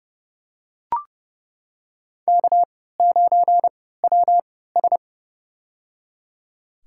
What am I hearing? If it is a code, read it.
Morse code, heard as a keyed tone, about 30 wpm, spaced as for 17 wpm, 700 Hz. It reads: K9WS
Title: USA Call Signs - 4 Characters - 3x Character Spacing - 30wpm